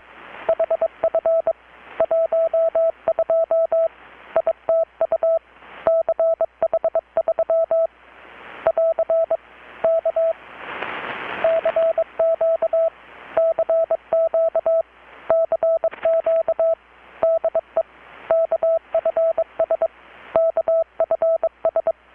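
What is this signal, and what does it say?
Morse code, heard as a keyed tone, about 23 wpm, 660 Hz. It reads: HF 12 ITU CH3 <AR> K CQ CQ CQ DE KFS KFS
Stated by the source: Morse from KPH's Night of Nights XX maritime radiotelegraphy event, received in Chiapas, Mexico about 3,800 km away on a Kenwood R-600 with a grounded 10 m long-wire antenna.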